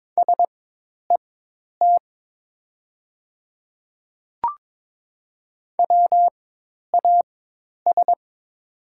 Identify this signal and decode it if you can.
Morse code, heard as a keyed tone, about 22 wpm, spaced as for 10 wpm, 690 Hz. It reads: SET WAS
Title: 4x Character Spacing - Top 200 Words - 22wpm